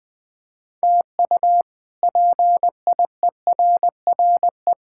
Morse code, heard as a keyed tone, about 20 wpm, 700 Hz. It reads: TU PIERRE
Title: K1USN SST QSO Practice - 20wpm